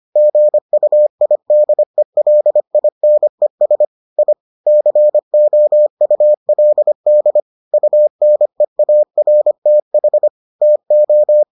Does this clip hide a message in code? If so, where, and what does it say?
Yes: Morse code from 0.2 to 11.4 s, heard as a keyed tone, about 25 wpm, 600 Hz. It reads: GUIDELINES I COULD UNEARTH TO